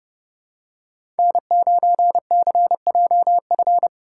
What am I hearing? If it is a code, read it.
Morse code, heard as a keyed tone, about 30 wpm, 700 Hz. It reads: N9CJF
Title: USA Call Signs - 5 Characters - 2x Character Spacing - 30wpm